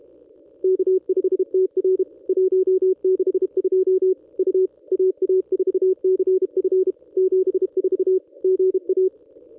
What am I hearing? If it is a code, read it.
Morse code, heard as a keyed tone, about 35 wpm, 410 Hz. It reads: K5TR 162 U AA4CF 74 GA
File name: contests / ss-cw-k5tr-op-n5rz -1200